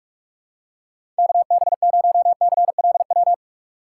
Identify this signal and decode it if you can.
Morse code, heard as a keyed tone, about 46 wpm, 700 Hz. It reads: KB0CLW
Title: USA Call Signs - 6 Characters - 3x Character Spacing - 45wpm